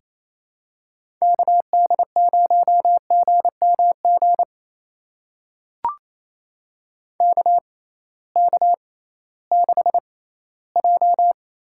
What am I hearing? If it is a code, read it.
Morse code, heard as a keyed tone, about 27 wpm, spaced as for 16 wpm, 700 Hz. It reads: KD0GMG KK6J
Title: USA Call Signs - 6 Characters - 6x Character Spacing - 28wpm